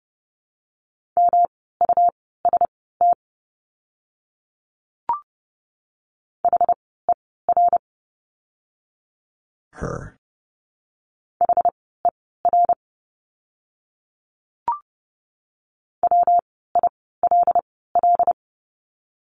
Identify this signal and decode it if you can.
Morse code, heard as a keyed tone, about 30 wpm, spaced as for 17 wpm, 700 Hz. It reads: MUST HER HER WILL
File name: course / 30:10wpm / 237 Lesson 38f - Letter q - Words 30wpm